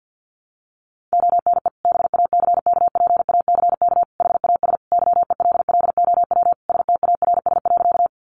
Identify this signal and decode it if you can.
Morse code, heard as a keyed tone, about 49 wpm, 700 Hz. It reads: ONE BACKPACK HAS YELLOW STARS.
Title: Additional Easy Phrases 50wpm